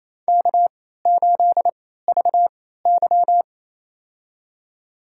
Morse code, heard as a keyed tone, about 28 wpm, spaced as for 16 wpm, 700 Hz. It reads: K8VY
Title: USA Call Signs - 3x Character Spacing - 28wpm